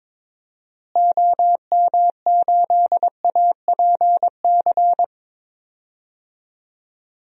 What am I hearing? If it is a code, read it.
Morse code, heard as a keyed tone, about 22 wpm, 700 Hz. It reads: OM8APC
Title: DXCC-100 Call Signs - 6 Characters - Repeated 3x - 22wpm